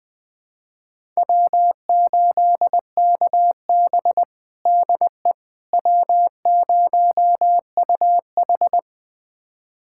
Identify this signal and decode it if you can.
Morse code, heard as a keyed tone, about 20 wpm, 700 Hz. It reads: W8KB DE W0UH
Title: USA Callsign DE Callsign - 4 Characters - 20wpm